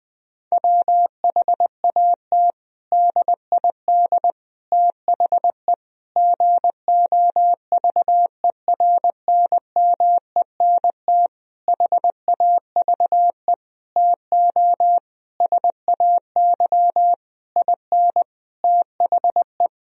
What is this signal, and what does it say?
Morse code, heard as a keyed tone, about 20 wpm, 700 Hz. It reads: WHAT DID THE GOVERNMENT HAVE TO SAY IN THE
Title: Sentences from Top 1000 Words - Review Part 3 - 20wpm